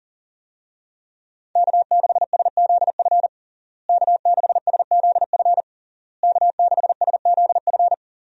Morse code, heard as a keyed tone, about 40 wpm, 700 Hz. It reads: K6SZF K6SZF K6SZF
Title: USA Call Signs - 5 Characters - Repeated 3x - 40wpm